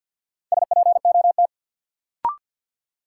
Morse code, heard as a keyed tone, about 46 wpm, 700 Hz. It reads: SPOT
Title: Sentences from Top 1100 Words - New Word Drill Alt - 50wpm